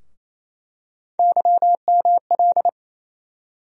Morse code, heard as a keyed tone, about 28 wpm, 700 Hz. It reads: YML
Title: USA Call Sign Postfixes - 3 Letters 28wpm